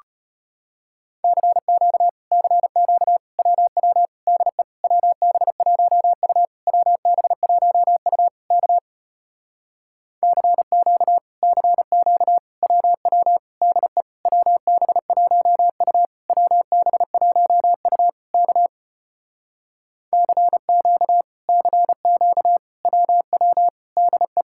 Morse code, heard as a keyed tone, about 33 wpm, 700 Hz. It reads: CQ CQ WW DE WB1U WB1U K CQ CQ WW DE WB1U WB1U K CQ CQ WW DE